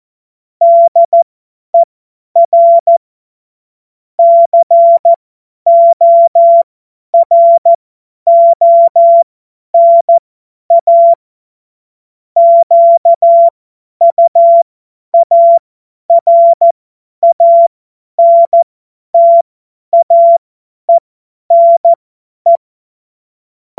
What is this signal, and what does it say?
Morse code, heard as a keyed tone, about 14 wpm, spaced as for 10 wpm, 680 Hz. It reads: DER CORONA QUARANTAENE
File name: Morsekurs / Witze / Witz 22. Juni 2021, 50 F70